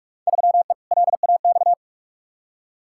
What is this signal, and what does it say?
Morse code, heard as a keyed tone, about 41 wpm, 700 Hz. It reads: 3E LAX
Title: Field Day Exchanges Encore - 45wpm